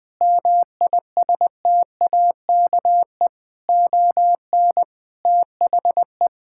Morse code, heard as a keyed tone, about 20 wpm, 700 Hz. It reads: MISTAKE ON THE
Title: Sentences with 5-Word Phrases - 20wpm